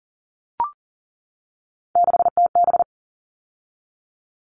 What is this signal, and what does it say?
Morse code, heard as a keyed tone, about 40 wpm, 700 Hz. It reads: 6TB
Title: All Cut Numbers - 3-Digit 40wpm